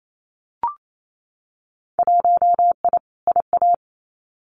Morse code, heard as a keyed tone, about 28 wpm, 700 Hz. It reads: 1I IA
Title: Winter Field Day Exchanges 28wpm